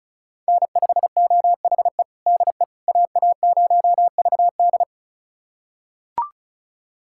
Morse code, heard as a keyed tone, about 36 wpm, 700 Hz. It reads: N5OHE DE AA0VD
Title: USA Callsign DE Callsign - 5 Characters - 35wpm